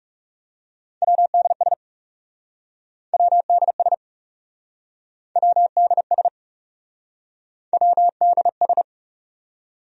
Morse code, heard as a keyed tone, about 40 wpm, 700 Hz. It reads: WDS WDS WDS WDS